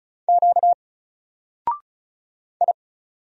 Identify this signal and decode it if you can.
Morse code, heard as a keyed tone, about 34 wpm, 700 Hz. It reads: Q I
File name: Single-Letter-Number-35wpm